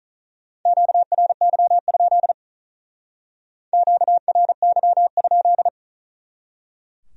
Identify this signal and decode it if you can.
Morse code, heard as a keyed tone, about 41 wpm, 700 Hz. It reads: QRY? QRY?
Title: Speed-Racing Q-Codes - 35wpm